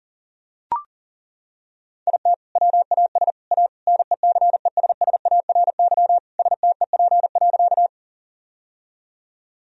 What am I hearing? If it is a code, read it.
Morse code, heard as a keyed tone, about 37 wpm, 700 Hz. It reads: IT WAS A NECESSARY STEP.
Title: Sentences from Top 1000 Words - Review Part 2 - 40wpm